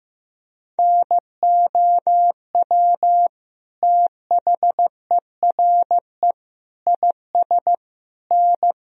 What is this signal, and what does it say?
Morse code, heard as a keyed tone, about 15 wpm, 700 Hz. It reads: NOW THERE IS N
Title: Sentences from Top 100 Words and Abbreviations 15wpm